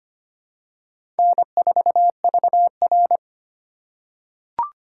Morse code, heard as a keyed tone, about 25 wpm, 700 Hz. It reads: N4VR
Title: USA Call Signs - 4 Characters - 3x Character Spacing - 25wpm